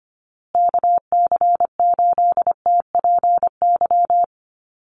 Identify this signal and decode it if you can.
Morse code, heard as a keyed tone, about 25 wpm, 700 Hz. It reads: KC8TPY
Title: USA Call Signs Repeated 3x 25wpm